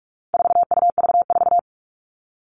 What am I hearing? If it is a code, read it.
Morse code, heard as a keyed tone, about 44 wpm, 700 Hz. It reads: 4UV4
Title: All Cut Numbers - 4-Digit 45wpm